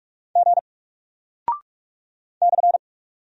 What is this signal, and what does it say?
Morse code, heard as a keyed tone, about 44 wpm, 700 Hz. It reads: G /
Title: Single Character (Letter:Number:Punctuation:Prosign) 45wpm